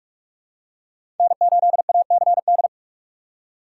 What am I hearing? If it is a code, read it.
Morse code, heard as a keyed tone, about 46 wpm, 700 Hz. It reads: N8ACD
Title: USA Call Signs - 5 Characters - 2x Character Spacing - 45wpm